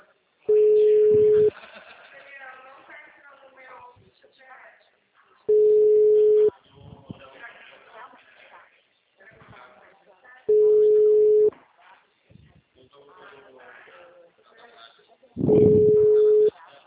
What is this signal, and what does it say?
Ringback tone (Europe, 425 Hz), with speech.